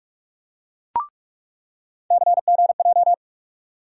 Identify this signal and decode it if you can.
Morse code, heard as a keyed tone, about 46 wpm, 700 Hz. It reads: CGJ
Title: USA Call Sign Postfixes - 3 Letters 45wpm